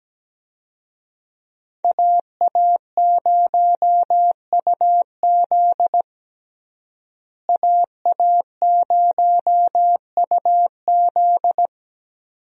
Morse code, heard as a keyed tone, about 17 wpm, 700 Hz. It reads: AA0UZ AA0UZ
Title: USA Call Signs - 5 Characters - Repeated 2x - 17wpm